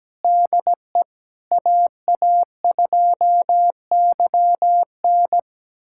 Morse code, heard as a keyed tone, about 17 wpm, 700 Hz. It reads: DE AA2YN
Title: USA Callsign DE Callsign - 5 Characters - 17wpm